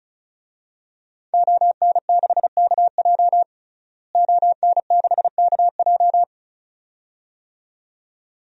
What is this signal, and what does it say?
Morse code, heard as a keyed tone, about 36 wpm, 700 Hz. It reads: ON6KJ ON6KJ
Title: DXCC-100 Call Signs - 5 Characters - Repeated 2x - 35wpm